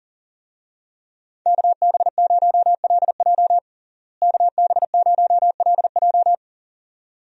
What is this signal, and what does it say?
Morse code, heard as a keyed tone, about 40 wpm, 700 Hz. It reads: KB0LJ KB0LJ
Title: USA Call Signs - 5 Characters - Repeated 2x - 40wpm